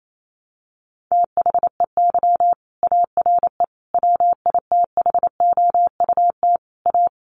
Morse code, heard as a keyed tone, about 28 wpm, 700 Hz. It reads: THEY ARE WITHOUT A
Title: Sentences from Top 1300 Words 28wpm